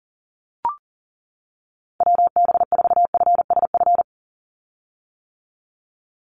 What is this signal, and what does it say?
Morse code, heard as a keyed tone, about 40 wpm, 700 Hz. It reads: WB4FSF
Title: Call Signs - 40wpm